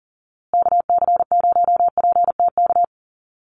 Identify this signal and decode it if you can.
Morse code, heard as a keyed tone, about 40 wpm, 700 Hz. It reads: KC0PTK
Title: USA Call Signs Repeated 2x 40wpm